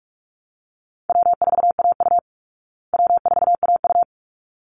Morse code, heard as a keyed tone, about 44 wpm, 700 Hz. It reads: W4AU W4AU
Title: Easy Call Signs Repeated 2x - 45wpm